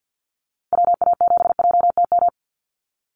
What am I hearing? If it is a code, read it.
Morse code, heard as a keyed tone, about 48 wpm, 700 Hz. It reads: WA7JTM
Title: NA SOTA Chaser Callsigns 50wpm